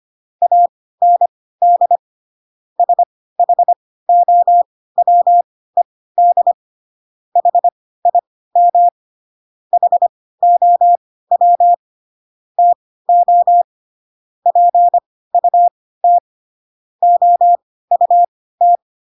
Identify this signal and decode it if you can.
Morse code, heard as a keyed tone, about 25 wpm, spaced as for 16 wpm, 700 Hz. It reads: AND SHOWED HIM HOW TO PUT OUT
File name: Pinocchio / Pinocchio ch3 10 25 ik1hge